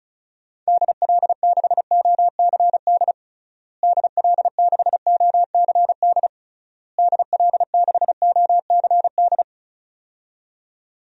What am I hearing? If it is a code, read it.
Morse code, heard as a keyed tone, about 36 wpm, 700 Hz. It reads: DL6OCD DL6OCD DL6OCD
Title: DXCC-100 Call Signs - 6 Characters - Repeated 3x - 35wpm